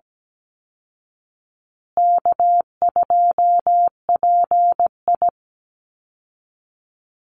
Morse code, heard as a keyed tone, about 17 wpm, 700 Hz. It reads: K2PI
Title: Easy Call Signs Repeated 2x - 17wpm